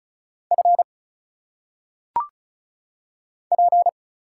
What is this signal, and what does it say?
Morse code, heard as a keyed tone, about 35 wpm, 700 Hz. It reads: F P
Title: Single Letters - Phonetic Pronunciation 35wpm